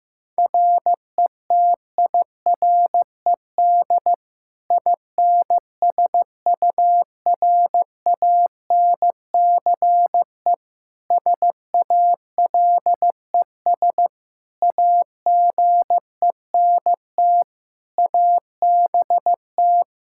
Morse code, heard as a keyed tone, about 15 wpm, 700 Hz. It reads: RETIRED INSURANCE SALES AGENT ABT
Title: Practice QSO - Exchange 3 - More Info - 15wpm